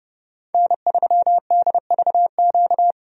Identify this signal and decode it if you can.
Morse code, heard as a keyed tone, about 30 wpm, 700 Hz. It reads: N3DVQ
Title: USA Call Signs - 5 Characters - 30wpm